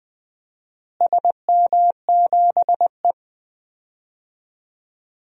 Morse code, heard as a keyed tone, about 20 wpm, 700 Hz. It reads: SM7E
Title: DXCC-100 Call Signs - 4 Characters - Repeated 3x - 20wpm